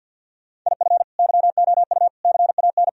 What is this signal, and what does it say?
Morse code, heard as a keyed tone, about 50 wpm, 700 Hz. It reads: IF YOU CAN
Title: Sentences from Top 1000 Words - Review Part 2 - 50wpm